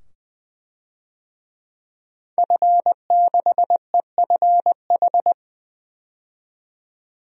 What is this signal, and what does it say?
Morse code, heard as a keyed tone, about 20 wpm, 700 Hz. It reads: F6EFH
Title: DXCC-100 Call Signs - 5 Characters - 20wpm